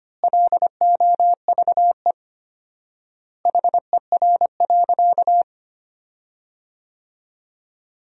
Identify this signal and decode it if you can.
Morse code, heard as a keyed tone, about 25 wpm, 700 Hz. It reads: LOVE HER.
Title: 4x Word-Spacing - Sentences from Top 400 Words - 25wpm